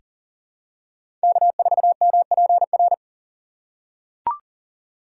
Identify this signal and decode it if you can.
Morse code, heard as a keyed tone, about 40 wpm, 700 Hz. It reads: K4MPR
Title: USA Call Signs - 5 Characters - 3x Character Spacing - 40wpm